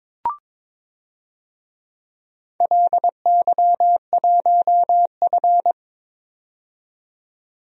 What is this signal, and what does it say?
Morse code, heard as a keyed tone, about 23 wpm, 700 Hz. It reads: LY1F
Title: DXCC-100 Call Signs - 4 Characters - 22wpm